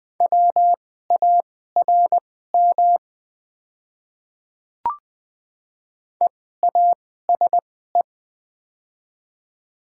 Morse code, heard as a keyed tone, about 20 wpm, spaced as for 14 wpm, 690 Hz. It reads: WARM EASE
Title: 2x Character Spacing - Top 500 Words - 20wpm